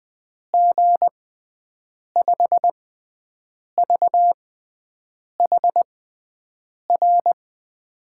Morse code, heard as a keyed tone, about 20 wpm, spaced as for 7 wpm, 700 Hz. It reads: G5VHR
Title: USA Call Signs - 6x Character Spacing - 20wpm